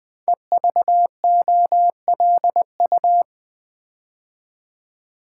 Morse code, heard as a keyed tone, about 20 wpm, 700 Hz. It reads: EVOLU
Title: Build A Word - 5 Letters At A Time - 12-Letter Words - 20wpm